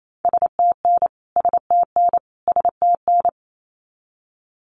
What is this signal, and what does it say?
Morse code, heard as a keyed tone, about 28 wpm, 700 Hz. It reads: STN STN STN
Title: Abbreviations Repeated 3x - 28wpm